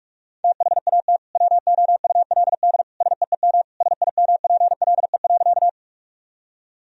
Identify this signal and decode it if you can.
Morse code, heard as a keyed tone, about 47 wpm, 700 Hz. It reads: THAT WOULD SEEM SIMPLE.
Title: Sentences from Top 1000 Words - Review Part 1 - 45wpm